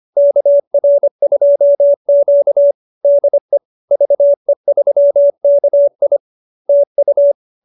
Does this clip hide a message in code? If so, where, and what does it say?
Yes: Morse code from 0.2 to 7.3 s, heard as a keyed tone, about 25 wpm, 590 Hz. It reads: KR2Q DE VE3KI TU